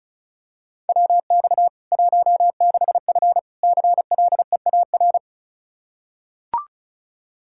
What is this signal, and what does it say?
Morse code, heard as a keyed tone, about 36 wpm, 700 Hz. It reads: WX 16F CLEAR
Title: Practice QSO - Exchange 2 - Info - 35wpm